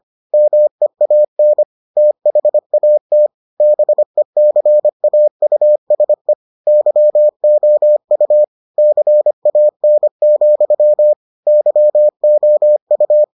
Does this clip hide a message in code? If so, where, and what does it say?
Yes: Morse code from 0.3 to 13.2 s, heard as a keyed tone, about 25 wpm, 600 Hz. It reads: MEAN THAT BECAUSE YOU CAN, YOU